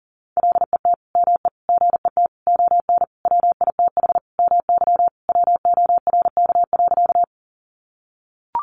Morse code, heard as a keyed tone, about 40 wpm, 700 Hz. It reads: LET ME GET ON WITH MY WORK.
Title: Sentences from Top 300 Words 40wpm